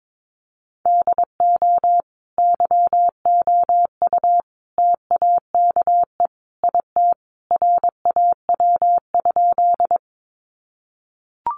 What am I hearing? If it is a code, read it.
Morse code, heard as a keyed tone, about 22 wpm, 700 Hz. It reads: DO YOU TAKE IT RAW?
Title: Sentences from Top 1500 Words 22wpm